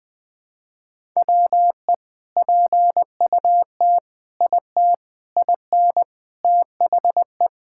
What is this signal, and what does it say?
Morse code, heard as a keyed tone, about 20 wpm, 700 Hz. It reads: WE PUT IT IN THE
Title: Sentences from Top 1000 Words - Review Part 5 - 20wpm